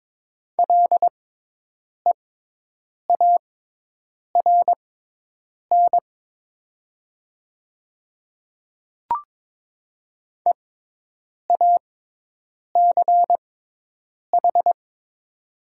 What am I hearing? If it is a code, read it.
Morse code, heard as a keyed tone, about 22 wpm, spaced as for 8 wpm, 700 Hz. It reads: LEARN EACH
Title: 6x Character Spacing - Top 300 Words - 22wpm